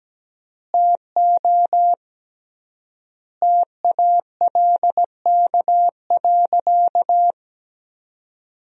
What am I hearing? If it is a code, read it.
Morse code, heard as a keyed tone, about 17 wpm, 700 Hz. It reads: TO TALK.
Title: 3x Word-Spacing - Sentences from Top 400 Words - 17wpm